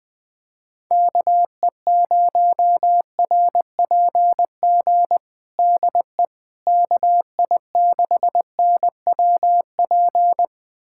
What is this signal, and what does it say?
Morse code, heard as a keyed tone, about 21 wpm, 700 Hz. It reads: KE0RPG DE KI6NWP
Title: USA Callsign DE Callsign - 6 Characters - 20wpm